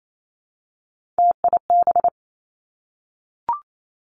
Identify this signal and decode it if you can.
Morse code, heard as a keyed tone, about 28 wpm, 700 Hz. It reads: TIB
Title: Three Challenging Letter-Number Combinations 28wpm